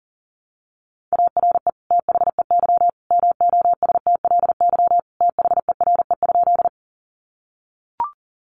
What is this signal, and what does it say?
Morse code, heard as a keyed tone, about 40 wpm, 700 Hz. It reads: ARE THEY MOSTLY THERE?
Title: Sentences from Top 1300 Words 40wpm